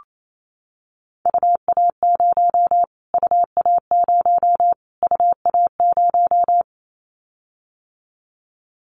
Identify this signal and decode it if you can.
Morse code, heard as a keyed tone, about 28 wpm, 700 Hz. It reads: UA0 UA0 UA0
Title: Callsign Prefixes for Top 25 DXCC - Repeated 3x - 28wpm